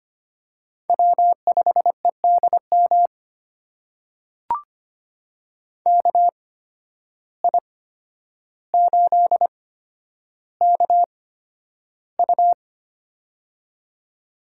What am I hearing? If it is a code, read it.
Morse code, heard as a keyed tone, about 25 wpm, spaced as for 11 wpm, 700 Hz. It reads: W5EDM KI8KU
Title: USA Call Signs - 5 Characters - 8x Character Spacing - 25wpm